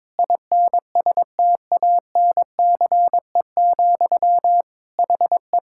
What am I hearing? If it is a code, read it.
Morse code, heard as a keyed tone, about 22 wpm, 700 Hz. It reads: INSTANCE, HE
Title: Sentences from Top 1000 Words - Review Part 5 - 22wpm